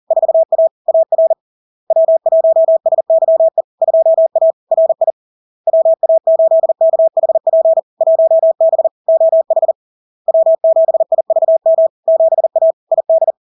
Morse code, heard as a keyed tone, about 40 wpm, 650 Hz. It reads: 4A AR W1SYE 2A RI WA8KHP 1B OH W7IVM 7A ID
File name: LARK-FD-2022-Log-in-CW-40wpm-650-Hz